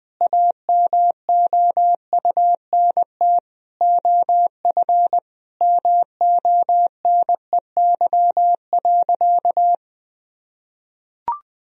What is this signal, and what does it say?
Morse code, heard as a keyed tone, about 20 wpm, 700 Hz. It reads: AMOUNT OF MONEY.
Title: Sentences from Top 1000 Words - Review Part 4 - 20wpm